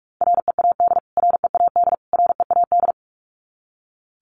Morse code, heard as a keyed tone, about 44 wpm, 700 Hz. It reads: READ READ READ
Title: Top 500 Words Repeated 45wpm